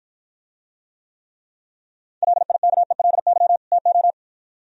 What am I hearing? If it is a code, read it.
Morse code, heard as a keyed tone, about 53 wpm, 700 Hz. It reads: LIKELY TO